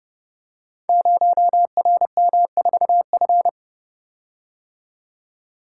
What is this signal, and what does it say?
Morse code, heard as a keyed tone, about 30 wpm, 700 Hz. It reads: 0RM4F
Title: Code Groups - 5 Letter:Numbers - 30wpm